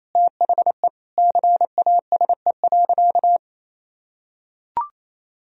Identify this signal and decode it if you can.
Morse code, heard as a keyed tone, about 28 wpm, 700 Hz. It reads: THE CASE.
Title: Sentences from Top 1000 Words - Review Part 2 - 28wpm